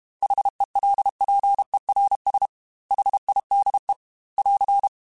Morse code, heard as a keyed tone, about 32 wpm, 800 Hz. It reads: HELPERS HIDE <AR>
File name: H Runs / H Runs mp3s / H Runs @30